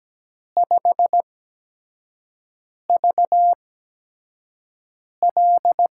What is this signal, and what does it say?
Morse code, heard as a keyed tone, about 17 wpm, spaced as for 5 wpm, 700 Hz. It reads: 5VL
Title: USA Call Signs - 6 Characters - 8x Character Spacing - 17wpm